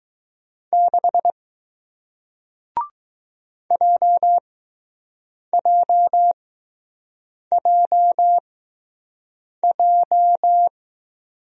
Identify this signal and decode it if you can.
Morse code, heard as a keyed tone, about 20 wpm, spaced as for 6 wpm, 700 Hz. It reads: 6 JJJJ